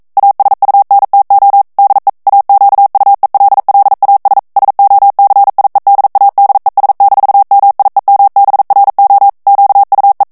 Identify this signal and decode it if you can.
Morse code, heard as a keyed tone, about 42 wpm, 800 Hz. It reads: ASUNTO DE AQUELLAS SOCIEDADES-MIEMBRO QUE
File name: morse mp3 / ESPAGNOL TEXTE EN CLAIR 1 VITESSE 40 MOTS MINUTES